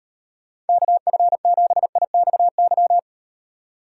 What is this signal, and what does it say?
Morse code, heard as a keyed tone, about 38 wpm, 700 Hz. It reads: KF7IXY